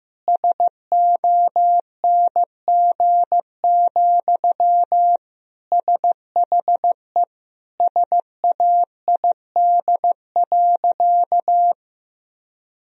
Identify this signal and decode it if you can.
Morse code, heard as a keyed tone, about 15 wpm, 700 Hz. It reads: SONG, SHE SAID.
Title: Sentences from Top 1000 Words - Review Part 2 - 15wpm